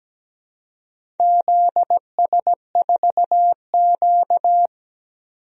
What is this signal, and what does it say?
Morse code, heard as a keyed tone, about 17 wpm, 700 Hz. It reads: ZS4Q
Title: DXCC-100 Call Signs - 4 Characters - 17wpm